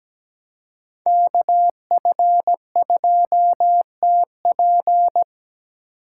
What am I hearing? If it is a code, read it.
Morse code, heard as a keyed tone, about 17 wpm, 700 Hz. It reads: KF2TP
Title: K1USN SST QSO Practice - 17wpm